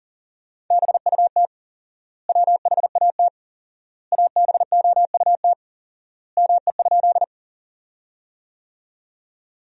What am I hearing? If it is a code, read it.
Morse code, heard as a keyed tone, about 40 wpm, 700 Hz. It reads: BUT WHAT ABOUT ME?
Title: 4x Word-Spacing - Sentences from Top 200 Words - 40wpm